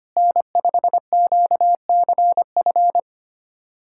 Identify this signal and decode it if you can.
Morse code, heard as a keyed tone, about 25 wpm, 700 Hz. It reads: N5QCF
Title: USA Callsign DE Callsign 25wpm